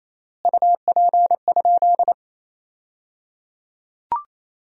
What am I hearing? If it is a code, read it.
Morse code, heard as a keyed tone, about 28 wpm, 700 Hz. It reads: UP?
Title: 3x Word-Spacing - Sentences from Top 500 Words - 28wpm